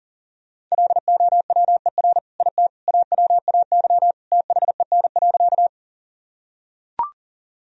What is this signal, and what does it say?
Morse code, heard as a keyed tone, about 40 wpm, 700 Hz. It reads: LOWER IT AWAY THEN.
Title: Sentences from Top 1000 Words - Review Part 4 - 40wpm